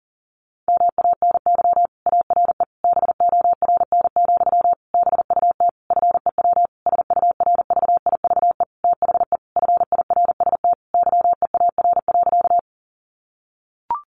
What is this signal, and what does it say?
Morse code, heard as a keyed tone, about 40 wpm, 700 Hz. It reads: MANY ARE BORN, BUT FEW SURVIVE THE FIRST YEAR.
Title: Sentences from Top 1300 Words 40wpm